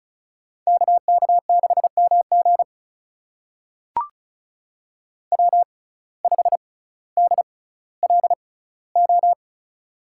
Morse code, heard as a keyed tone, about 36 wpm, spaced as for 18 wpm, 700 Hz. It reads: KK6MG W5DLO
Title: USA Call Signs - 5 Characters - 6x Character Spacing - 35wpm